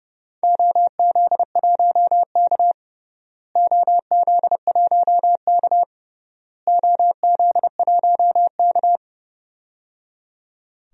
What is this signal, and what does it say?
Morse code, heard as a keyed tone, about 29 wpm, 700 Hz. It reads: OZ1K OZ1K OZ1K
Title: DXCC-100 Call Signs - 4 Characters - Repeated 3x - 30wpm